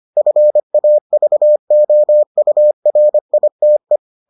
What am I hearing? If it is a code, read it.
Morse code, heard as a keyed tone, about 25 wpm, 600 Hz. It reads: FAVOURITE